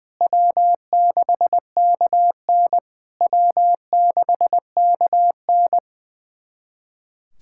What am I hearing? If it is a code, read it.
Morse code, heard as a keyed tone, about 20 wpm, 700 Hz. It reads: W6KN W6KN
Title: SOTA QSO Practice 20wpm